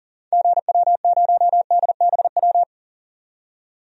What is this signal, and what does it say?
Morse code, heard as a keyed tone, about 37 wpm, 700 Hz. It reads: GW0DBW
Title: DXCC-100 Call Signs - 6 Characters - 40wpm